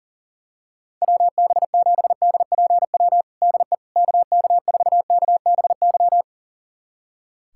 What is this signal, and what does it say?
Morse code, heard as a keyed tone, about 40 wpm, 700 Hz. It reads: WB7DPW DE KK4KBY
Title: USA Callsign DE Callsign - 6 Characters - 40wpm